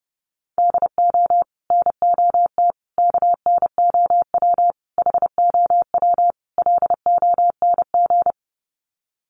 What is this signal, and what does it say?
Morse code, heard as a keyed tone, about 28 wpm, 700 Hz. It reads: DO NOT KNOW HOW LONG
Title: Sets of 5 Words Encore2 30wpm